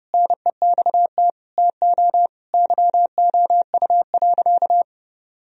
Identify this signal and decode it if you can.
Morse code, heard as a keyed tone, about 30 wpm, 700 Hz. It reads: NEXT TO YOU.
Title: Sentences from Top 1000 Words - Review Part 2 - 30wpm